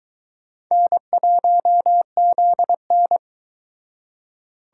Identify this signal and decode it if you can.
Morse code, heard as a keyed tone, about 22 wpm, 700 Hz. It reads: N1ZN